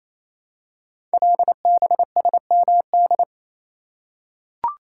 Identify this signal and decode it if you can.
Morse code, heard as a keyed tone, about 27 wpm, 700 Hz. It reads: LBSMD
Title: Code Groups - 5 Letters - 28wpm